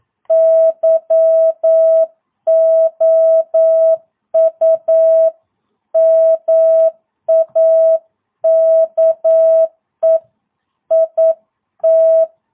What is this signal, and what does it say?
Morse code, heard as a keyed tone, about 9 wpm, 660 Hz. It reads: YOU MAKE IT
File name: jh8cbh / 2022cw